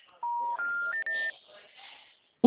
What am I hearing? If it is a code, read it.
Special information tone.